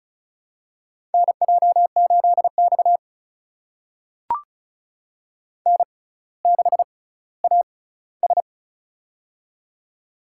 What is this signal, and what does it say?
Morse code, heard as a keyed tone, about 36 wpm, spaced as for 18 wpm, 700 Hz. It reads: NJ8X N6AS
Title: USA Call Signs - 4 Characters - 6x Character Spacing - 35wpm